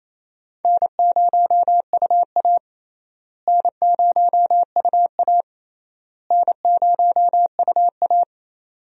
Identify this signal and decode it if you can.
Morse code, heard as a keyed tone, about 28 wpm, 700 Hz. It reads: N0UA N0UA N0UA
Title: USA Call Signs - 4 Characters - Repeated 3x - 28wpm